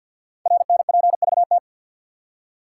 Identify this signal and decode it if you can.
Morse code, heard as a keyed tone, about 50 wpm, 700 Hz. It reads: RNPVT
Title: Code Groups - 5 Letters - 50wpm